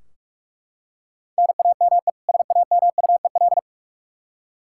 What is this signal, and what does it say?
Morse code, heard as a keyed tone, about 43 wpm, 700 Hz. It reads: NAME SAMUEL
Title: Practice QSO - Exchange 1 - Beginning - 45wpm